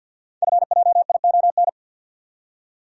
Morse code, heard as a keyed tone, about 46 wpm, 700 Hz. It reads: FJION
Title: Code Groups - 5 Letters - 50wpm